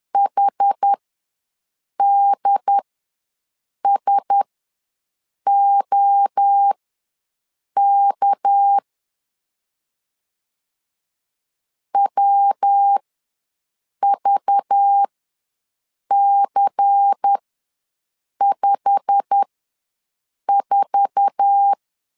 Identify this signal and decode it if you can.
Morse code, heard as a keyed tone, about 10 wpm, spaced as for 6 wpm, 800 Hz. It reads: HDSOK WVC54
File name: LEZIONI / 8 YJ45 / 8 YJ45 ws 10wpm 3843